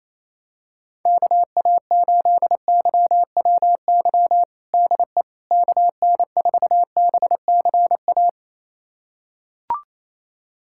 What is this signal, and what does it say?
Morse code, heard as a keyed tone, about 28 wpm, 700 Hz. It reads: KA8YWY DE KN4BCA
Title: USA Callsign DE Callsign - 6 Characters - 28wpm